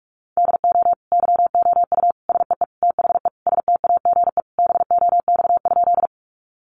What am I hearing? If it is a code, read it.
Morse code, heard as a keyed tone, about 44 wpm, 700 Hz. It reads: DO YOU SEE THE STAGE BOX?
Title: Sentences from Top 1200 Words 45wpm